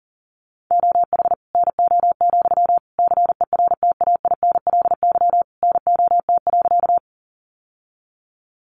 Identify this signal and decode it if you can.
Morse code, heard as a keyed tone, about 40 wpm, 700 Hz. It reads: OH NO, CERTAINLY NOT.
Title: Sentences from Top 900 Words 40wpm